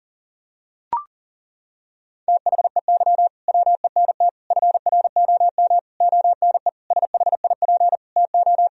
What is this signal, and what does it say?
Morse code, heard as a keyed tone, about 40 wpm, 700 Hz. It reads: THEY WENT FROM ONE SHIP TO
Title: Sentences from Top 1000 Words - Review Part 2 - 40wpm